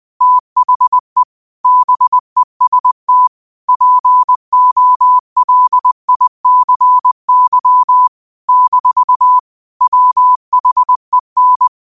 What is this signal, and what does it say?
Morse code, heard as a keyed tone, about 20 wpm, 1000 Hz. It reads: THE BEST POLICY - WHEN
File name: qotdfiles / 20wpm